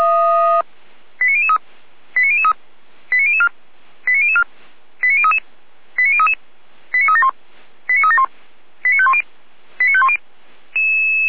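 Five-tone selective call (ZVEI), heard as quick runs of five tones, tones of about 70 ms. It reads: 89003 89003 89004 89004 89033 89033 89371 89371 89510 89510
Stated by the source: ZVEI five-tone selective call received by SDR, logged on a Feuerwehr Orlamünde fire brigade server.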